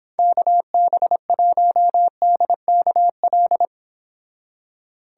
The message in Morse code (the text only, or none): KB1DKL